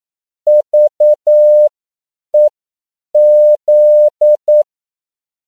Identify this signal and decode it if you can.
Morse code, heard as a keyed tone, about 9 wpm, spaced as for 7 wpm, 600 Hz. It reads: VEZ